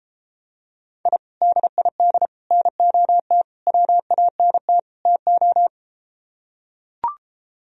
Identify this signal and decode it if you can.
Morse code, heard as a keyed tone, about 33 wpm, 700 Hz. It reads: I DID NOT WANT TO